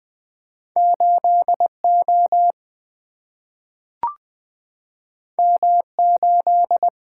Morse code, heard as a keyed tone, about 20 wpm, 700 Hz. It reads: 8 O M 8
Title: Two Letter-Number Combinations - 20wpm